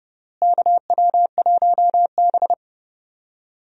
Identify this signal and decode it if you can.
Morse code, heard as a keyed tone, about 30 wpm, 700 Hz. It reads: KW1B
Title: USA Call Signs - 4 Characters - Repeated 3x - 30wpm